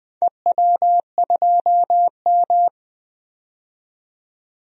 Morse code, heard as a keyed tone, about 20 wpm, 700 Hz. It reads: EW2M
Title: DXCC-100 Call Signs - 4 Characters - 20wpm